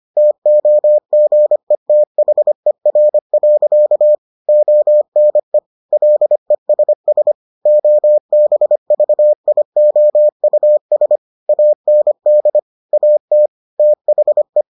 Morse code, heard as a keyed tone, about 25 wpm, 600 Hz. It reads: TOGETHER. ONE LESS OBVIOUS AND AT THE